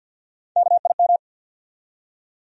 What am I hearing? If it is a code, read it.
Morse code, heard as a keyed tone, about 53 wpm, 700 Hz. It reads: KIM
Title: USA Common Names - Max 4 Letters - 50wpm